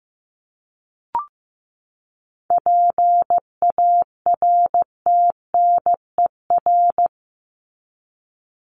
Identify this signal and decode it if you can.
Morse code, heard as a keyed tone, about 15 wpm, 700 Hz. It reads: PARTNER
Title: Top 1501-1750 Words 15wpm